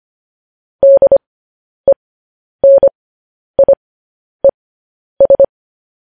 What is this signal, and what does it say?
Morse code, heard as a keyed tone, about 25 wpm, spaced as for 10 wpm, 550 Hz. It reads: DENIES
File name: podcast / News headlines-1x-25x10